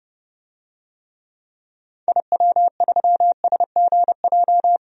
Morse code, heard as a keyed tone, about 30 wpm, 700 Hz. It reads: IW3SGJ
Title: DXCC-100 Call Signs - 6 Characters - 30wpm